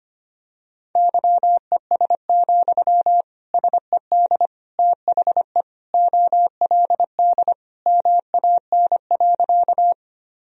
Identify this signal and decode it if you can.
Morse code, heard as a keyed tone, about 25 wpm, 700 Hz. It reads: YES, SED THE OLD MAN.
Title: Sentences from Top 500 Words and Abbreviations 25wpm